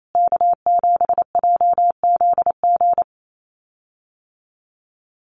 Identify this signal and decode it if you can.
Morse code, heard as a keyed tone, about 28 wpm, 700 Hz. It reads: K7JZG
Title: USA Call Signs Repeated 3x 28wpm